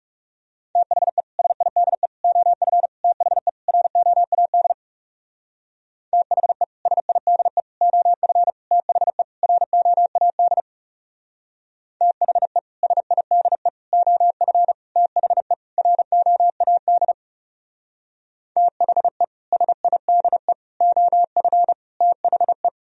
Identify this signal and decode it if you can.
Morse code, heard as a keyed tone, about 36 wpm, 700 Hz. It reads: THE SIDE OF THE ROAD THE SIDE OF THE ROAD THE SIDE OF THE ROAD THE SIDE OF THE
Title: Speed-Racing - Sets of 5 Words 30wpm